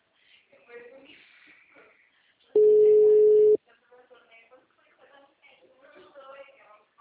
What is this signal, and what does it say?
Ringback tone (Europe, 425 Hz), with speech.
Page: Valinhos - SP